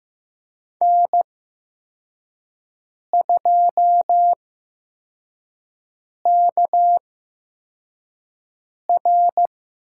Morse code, heard as a keyed tone, about 15 wpm, spaced as for 4 wpm, 700 Hz. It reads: N2KR
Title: USA Call Signs - 8x Character Spacing - 15wpm